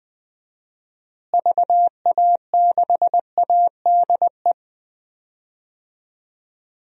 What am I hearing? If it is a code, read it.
Morse code, heard as a keyed tone, about 20 wpm, 700 Hz. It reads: VA6ADE
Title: DXCC-100 Call Signs - 6 Characters - Repeated 2x - 20wpm